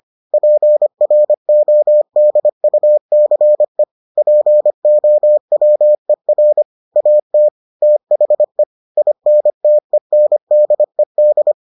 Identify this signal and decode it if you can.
Morse code, heard as a keyed tone, about 25 wpm, 600 Hz. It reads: PRODUCE POWER AT THE INTENDED